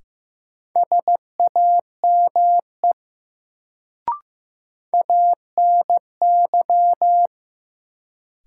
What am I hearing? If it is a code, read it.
Morse code, heard as a keyed tone, about 15 wpm, 700 Hz. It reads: SAME ANY
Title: Sentences from Top 200 Words - New Word Drill - 15wpm